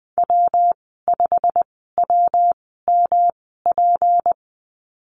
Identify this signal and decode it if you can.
Morse code, heard as a keyed tone, about 20 wpm, spaced as for 14 wpm, 700 Hz. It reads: W5WMP
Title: 186 Lesson 30g - Letter b - Call Signs 20wpm